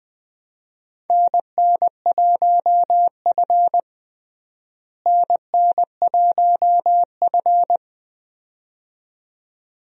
Morse code, heard as a keyed tone, about 20 wpm, 700 Hz. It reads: NN1F NN1F
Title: USA Call Signs - 4 Characters - Repeated 2x - 20wpm